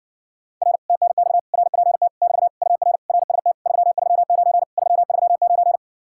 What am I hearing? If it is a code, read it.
Morse code, heard as a keyed tone, about 60 wpm, 700 Hz. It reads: R TNX RPT <BT> UR RST 339 339